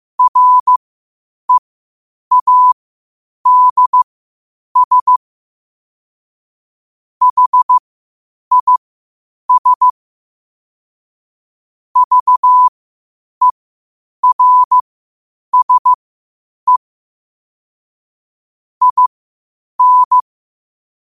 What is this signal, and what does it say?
Morse code, heard as a keyed tone, about 15 wpm, spaced as for 8 wpm, 1000 Hz. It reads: READS HIS VERSE IN
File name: qotdfiles / 8wpm